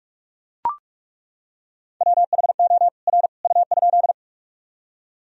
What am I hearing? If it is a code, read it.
Morse code, heard as a keyed tone, about 41 wpm, 700 Hz. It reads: WHO R U?
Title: Sentences from Top 100 Words and Abbreviations 45wpm